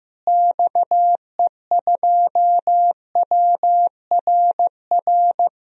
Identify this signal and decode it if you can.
Morse code, heard as a keyed tone, about 15 wpm, 700 Hz. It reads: XE2WRR
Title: DXCC-100 Call Signs - 6 Characters - Repeated 2x - 15wpm